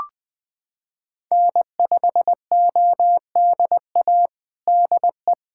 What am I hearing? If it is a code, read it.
Morse code, heard as a keyed tone, about 20 wpm, 700 Hz. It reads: N5ODA DE
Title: USA Callsign DE Callsign - 5 Characters - 20wpm